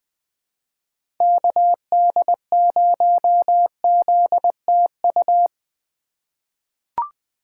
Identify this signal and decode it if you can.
Morse code, heard as a keyed tone, about 20 wpm, 700 Hz. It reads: KD0ZTU
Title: USA Call Signs - 6 Characters - 3x Character Spacing - 20wpm